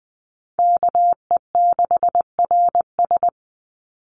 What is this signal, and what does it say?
Morse code, heard as a keyed tone, about 20 wpm, 700 Hz. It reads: KE6RS
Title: NA SOTA Chaser Callsigns 20wpm